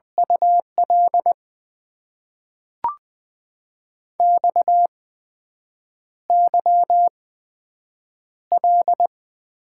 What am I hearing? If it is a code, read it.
Morse code, heard as a keyed tone, about 20 wpm, spaced as for 7 wpm, 700 Hz. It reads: UL XYL
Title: Top 90 QSO Elements - 8x Character Spacing - 20wpm